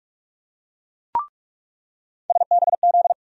Morse code, heard as a keyed tone, about 43 wpm, 700 Hz. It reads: SBZ
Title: USA Call Sign Postfixes - 3 Letters 45wpm